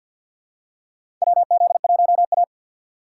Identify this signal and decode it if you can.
Morse code, heard as a keyed tone, about 53 wpm, 700 Hz. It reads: WZ1A
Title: USA Call Signs - 6x Character Spacing - 50wpm